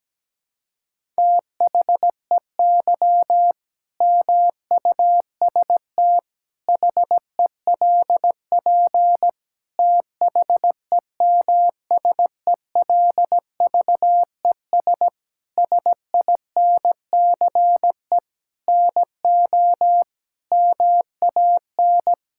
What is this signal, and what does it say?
Morse code, heard as a keyed tone, about 17 wpm, 700 Hz. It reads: THEY MUST HELP THEMSELVES SINCE NO MAN
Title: Sentences from Top 1000 Words - Review Part 5 - 17wpm